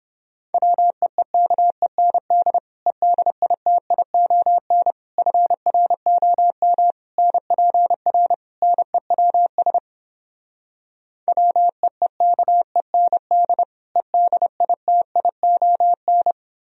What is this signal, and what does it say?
Morse code, heard as a keyed tone, about 29 wpm, 700 Hz. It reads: WEEKEND EDITION FROM NPR NEWS WEEKEND EDITION